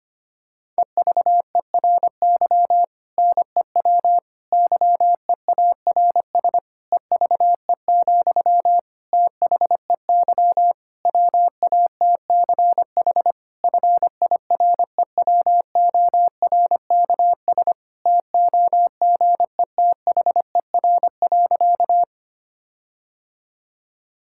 Morse code, heard as a keyed tone, about 25 wpm, 700 Hz. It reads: EVERY NEW YEARS EVE, THEY WATCH FIREWORKS TOGETHER.
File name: practice / 25wpm / Difficult Sentences with Hints - 25wpm